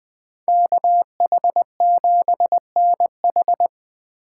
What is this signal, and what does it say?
Morse code, heard as a keyed tone, about 20 wpm, 700 Hz. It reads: KH7NH
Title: USA Call Signs - 8x Character Spacing - 20wpm